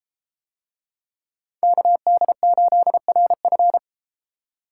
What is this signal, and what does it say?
Morse code, heard as a keyed tone, about 32 wpm, 700 Hz. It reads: KD8RF